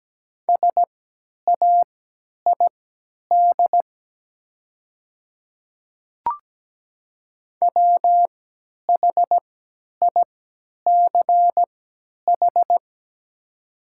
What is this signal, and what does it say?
Morse code, heard as a keyed tone, about 17 wpm, spaced as for 10 wpm, 700 Hz. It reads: SAID WHICH
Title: 3x Character Spacing - Top 100 Words - 17wpm